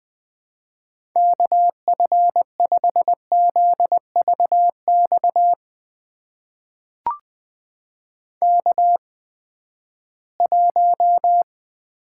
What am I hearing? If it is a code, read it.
Morse code, heard as a keyed tone, about 20 wpm, spaced as for 14 wpm, 700 Hz. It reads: KF5ZVX K1
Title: USA Call Signs - 8x Character Spacing - 20wpm